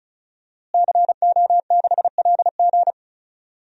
Morse code, heard as a keyed tone, about 36 wpm, 700 Hz. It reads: CO6LG
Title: DXCC-100 Call Signs - 5 Characters - Repeated 3x - 35wpm